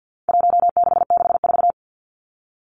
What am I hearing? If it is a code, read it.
Morse code, heard as a keyed tone, about 50 wpm, 700 Hz. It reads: 1664
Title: Numbers - Contest Style - 4-Digit 50wpm